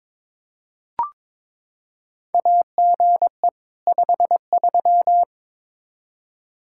Morse code, heard as a keyed tone, about 22 wpm, 700 Hz. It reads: AGE 53
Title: Practice QSO - Exchange 3 - More Info - 22wpm